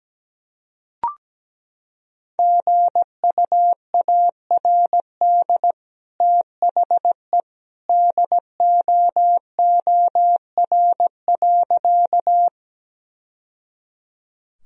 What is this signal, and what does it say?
Morse code, heard as a keyed tone, about 17 wpm, 700 Hz. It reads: GUARD THE DOOR.
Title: Sentences from Top 1000 Words - Review Part 4 - 17wpm